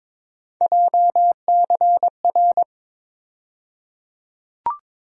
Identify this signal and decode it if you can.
Morse code, heard as a keyed tone, about 22 wpm, 700 Hz. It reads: JCR
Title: USA Call Sign Postfixes - 3 Letters 22wpm